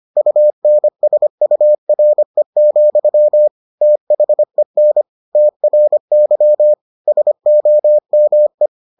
Morse code, heard as a keyed tone, about 25 wpm, 600 Hz. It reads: UNSURE, THEN TRY SOME